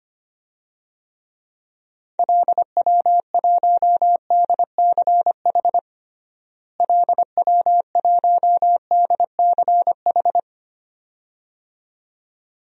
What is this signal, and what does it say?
Morse code, heard as a keyed tone, about 25 wpm, 700 Hz. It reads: LW1DCH LW1DCH
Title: DXCC-100 Call Signs - 6 Characters - Repeated 2x - 25wpm